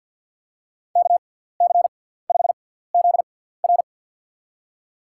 Morse code, heard as a keyed tone, about 50 wpm, spaced as for 17 wpm, 700 Hz. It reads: KC5ZR
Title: USA Call Signs - 5 Characters - 6x Character Spacing - 50wpm